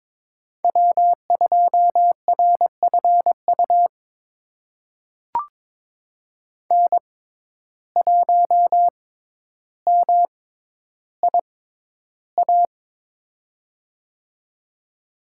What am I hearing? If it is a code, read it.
Morse code, heard as a keyed tone, about 22 wpm, spaced as for 11 wpm, 700 Hz. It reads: W2RFU N1MIA
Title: USA Call Signs - 5 Characters - 6x Character Spacing - 22wpm